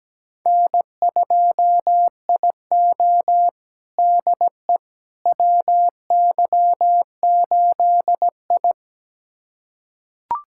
Morse code, heard as a keyed tone, about 17 wpm, 700 Hz. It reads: N2IO DE WY8I
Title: USA Callsign DE Callsign - 4 Characters - 17wpm